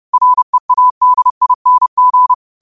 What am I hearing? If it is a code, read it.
Morse code, heard as a keyed tone, about 30 wpm, 1000 Hz. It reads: READING
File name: qotdfiles / 30wpm